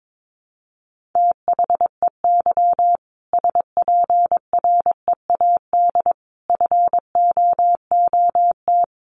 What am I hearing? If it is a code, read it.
Morse code, heard as a keyed tone, about 22 wpm, 700 Hz. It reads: THEY SPREAD FOOT
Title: Sentences from Top 1500 Words 22wpm